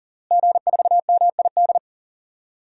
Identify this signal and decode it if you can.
Morse code, heard as a keyed tone, about 40 wpm, 700 Hz. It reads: G4MID
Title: DXCC-100 Call Signs - 5 Characters - Repeated 3x - 40wpm